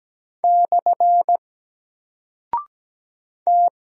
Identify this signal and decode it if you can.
Morse code, heard as a keyed tone, about 17 wpm, 700 Hz. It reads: / T
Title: Single Character (Letter:Number:Punctuation:Prosign) Alt 17wpm